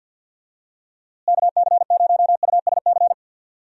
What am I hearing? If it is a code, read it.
Morse code, heard as a keyed tone, about 51 wpm, 700 Hz. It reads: KC0USC
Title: USA Call Signs - 6 Characters - Repeated 3x - 50wpm